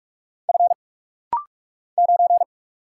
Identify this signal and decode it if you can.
Morse code, heard as a keyed tone, about 44 wpm, 700 Hz. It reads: F 9
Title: Single Letter-Number Alt - Rapid-Fire 45wpm